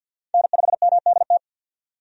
Morse code, heard as a keyed tone, about 51 wpm, 700 Hz. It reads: N5MDT
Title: USA Call Signs - 5 Characters - 50wpm